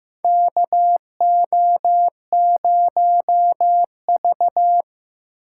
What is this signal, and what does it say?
Morse code, heard as a keyed tone, about 15 wpm, 700 Hz. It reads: KO0V
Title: USA Call Signs - 3x Character Spacing - 15wpm